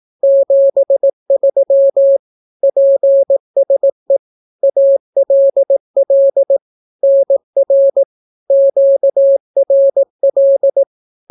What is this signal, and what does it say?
Morse code, heard as a keyed tone, about 18 wpm, 550 Hz. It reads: 73 PSE ALL NR QRL
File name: naukapodstaw / skroty / skroty